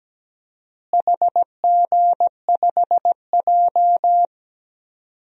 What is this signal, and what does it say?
Morse code, heard as a keyed tone, about 17 wpm, 700 Hz. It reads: HG5J
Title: DXCC-100 Call Signs - 4 Characters - Repeated 3x - 17wpm